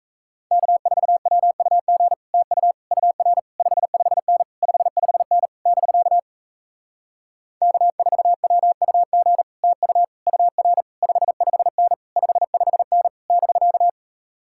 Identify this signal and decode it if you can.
Morse code, heard as a keyed tone, about 37 wpm, 700 Hz. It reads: K4WUG TU UR 55N 55N <BK> K4WUG TU UR 55N 55N <BK>